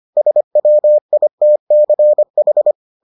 Morse code, heard as a keyed tone, about 25 wpm, 600 Hz. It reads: SWITCH